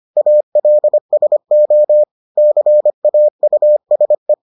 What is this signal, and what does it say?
Morse code, heard as a keyed tone, about 25 wpm, 600 Hz. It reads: ALSO CAUSE